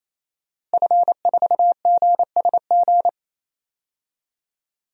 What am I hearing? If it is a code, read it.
Morse code, heard as a keyed tone, about 28 wpm, 690 Hz. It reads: F4GSG
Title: DXCC-100 Call Signs - 5 Characters - Repeated 3x - 28wpm